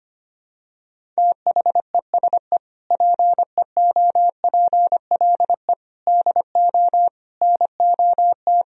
Morse code, heard as a keyed tone, about 25 wpm, 700 Hz. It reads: THESE PEOPLE DO NOT
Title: Sentences from Top 1000 Words - Review Part 1 - 25wpm